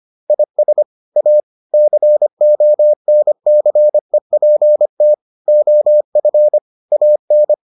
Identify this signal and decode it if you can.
Morse code, heard as a keyed tone, about 25 wpm, 600 Hz. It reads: IS A CONCEPT OF AN